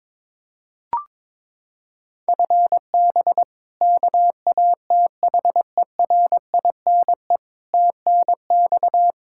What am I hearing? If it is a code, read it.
Morse code, heard as a keyed tone, about 22 wpm, 700 Hz. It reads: FB KATHERINE TNX
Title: Practice QSO - Exchange 3 - More Info - 22wpm